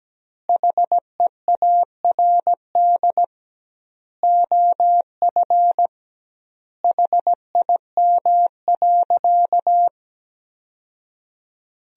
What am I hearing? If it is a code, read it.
Morse code, heard as a keyed tone, about 17 wpm, 700 Hz. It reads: HEARD OF HIM.
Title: 2x Word-Spacing - Sentences from Top 400 Words - 17wpm